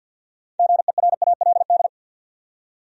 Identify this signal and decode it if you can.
Morse code, heard as a keyed tone, about 51 wpm, 700 Hz. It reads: GERALD
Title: USA Common Names - 50wpm